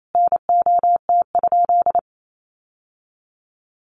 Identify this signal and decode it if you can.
Morse code, heard as a keyed tone, about 28 wpm, 700 Hz. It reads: NOT?
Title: Sentences from Top 1000 Words 28wpm